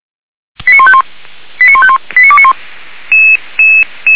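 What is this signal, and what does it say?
Five-tone selective call (ZVEI), heard as quick runs of five tones, tones of about 70 ms. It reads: 89151 89151 89291
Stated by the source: ZVEI five-tone selective call 89485 received on SDR.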